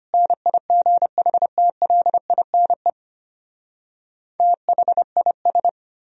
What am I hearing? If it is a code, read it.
Morse code, heard as a keyed tone, about 33 wpm, 700 Hz. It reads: NIGHTLINE THIS